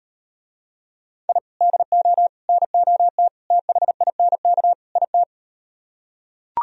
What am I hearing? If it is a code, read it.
Morse code, heard as a keyed tone, about 37 wpm, 700 Hz. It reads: I DO NOT THINK IT